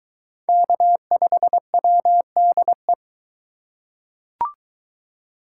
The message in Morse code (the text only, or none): K5WDE